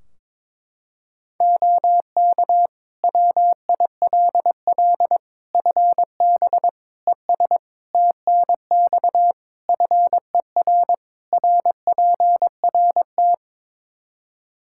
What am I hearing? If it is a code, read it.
Morse code, heard as a keyed tone, about 22 wpm, 700 Hz. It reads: OK WILL FB ES TNX FER RPRT
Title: Practice QSO - Exchange 2 - Info - 22wpm